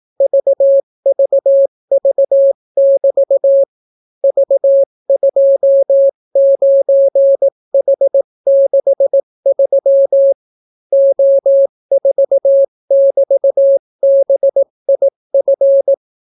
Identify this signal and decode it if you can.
Morse code, heard as a keyed tone, about 18 wpm, 550 Hz. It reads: VVV<BT> V29H63 O4<BT>BIF